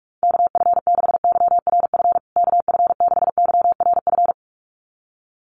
Morse code, heard as a keyed tone, about 44 wpm, 700 Hz. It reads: KF6YRF KF6YRF
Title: USA Call Signs Repeated 2x 45wpm